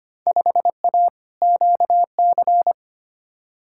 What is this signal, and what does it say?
Morse code, heard as a keyed tone, about 25 wpm, 700 Hz. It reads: 5A QC
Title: Field Day Exchanges Encore - 25wpm